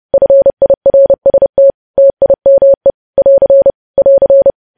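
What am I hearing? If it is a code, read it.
Morse code, heard as a keyed tone, about 30 wpm, 550 Hz. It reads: FIRST TIME <AR> <AR>
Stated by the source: Morse news headlines sent at 30 wpm, opening with vvv.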